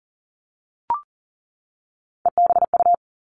Morse code, heard as a keyed tone, about 40 wpm, 700 Hz. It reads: EBU
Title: All Cut Numbers - 3-Digit 40wpm